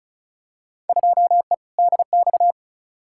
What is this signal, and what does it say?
Morse code, heard as a keyed tone, about 34 wpm, 700 Hz. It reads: 2E DX
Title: Field Day Exchanges - 35wpm